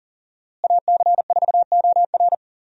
Morse code, heard as a keyed tone, about 39 wpm, 700 Hz. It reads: AC4OR
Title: USA Call Signs - 5 Characters - Repeated 3x - 40wpm